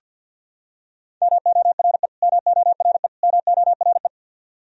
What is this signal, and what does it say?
Morse code, heard as a keyed tone, about 48 wpm, 700 Hz. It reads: MORE MORE MORE
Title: Top 100 Words Repeated 50wpm